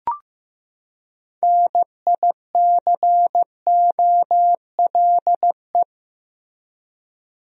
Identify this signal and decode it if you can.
Morse code, heard as a keyed tone, about 15 wpm, 700 Hz. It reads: NICOLE
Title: USA Common Names - 15wpm